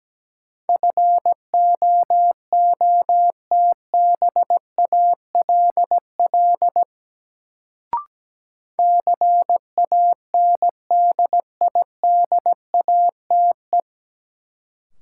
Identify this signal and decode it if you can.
Morse code, heard as a keyed tone, about 17 wpm, 700 Hz. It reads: FOOTBALL CANDIDATE
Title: Sentences from Top 1600 Words - New Word Drill Alt - 17wpm